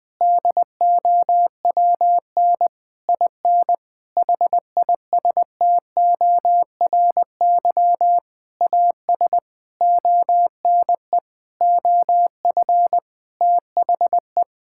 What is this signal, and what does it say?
Morse code, heard as a keyed tone, about 20 wpm, 700 Hz. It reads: DOWN IN HISTORY AS ONE OF THE
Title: Sentences with 8-Word Phrases - 20wpm